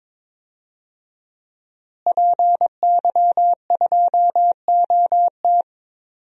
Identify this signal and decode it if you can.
Morse code, heard as a keyed tone, about 22 wpm, 700 Hz. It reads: PY2OT
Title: DXCC-100 Call Signs - 5 Characters - Repeated 3x - 22wpm